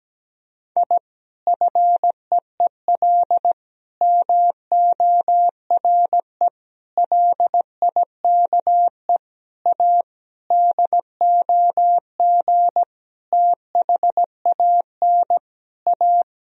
Morse code, heard as a keyed tone, about 17 wpm, 700 Hz. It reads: I FEEL MORE LIKE A DOG THAN A
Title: Sentences from Top 1000 Words - Review Part 2 - 17wpm